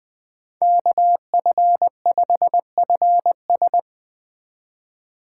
Morse code, heard as a keyed tone, about 20 wpm, 700 Hz. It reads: KF5FS
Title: USA Call Signs - 8x Character Spacing - 20wpm